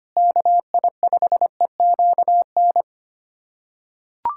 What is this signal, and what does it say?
Morse code, heard as a keyed tone, about 25 wpm, 700 Hz. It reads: KI5EQN
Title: USA Callsign DE Callsign - 6 Characters - 25wpm